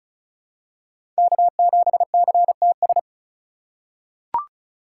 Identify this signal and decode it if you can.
Morse code, heard as a keyed tone, about 35 wpm, 700 Hz. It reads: K7CTS
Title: USA Call Signs - 5 Characters - 2x Character Spacing - 35wpm